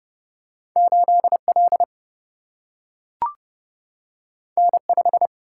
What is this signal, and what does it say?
Morse code, heard as a keyed tone, about 31 wpm, 700 Hz. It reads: 8 L N 5
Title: Single-Letter Single-Number 30wpm